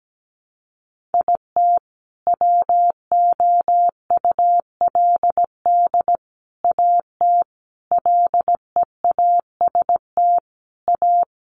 Morse code, heard as a keyed tone, about 17 wpm, 700 Hz. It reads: IT WOULD AT LEAST A
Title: Sentences from Top 700 Words 17wpm